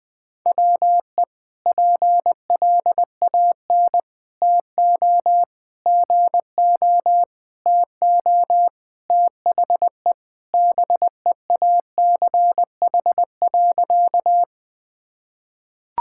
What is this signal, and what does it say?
Morse code, heard as a keyed tone, about 20 wpm, 700 Hz. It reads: WE PLAN TO GO TO THE BEACH.
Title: Sentences with 4-Word Phrases - 20wpm